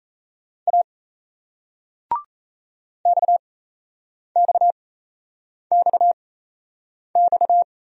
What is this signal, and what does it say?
Morse code, heard as a keyed tone, about 38 wpm, spaced as for 8 wpm, 700 Hz. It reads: A XXXX